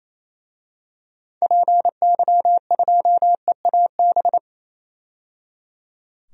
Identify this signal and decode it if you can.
Morse code, heard as a keyed tone, about 28 wpm, 700 Hz. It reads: PY2EAB
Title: DXCC-100 Call Signs - 6 Characters - 28wpm